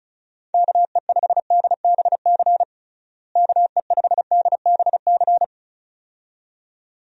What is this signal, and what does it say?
Morse code, heard as a keyed tone, about 34 wpm, 700 Hz. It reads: KE5DBC KE5DBC
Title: USA Call Signs - 6 Characters - Repeated 2x - 35wpm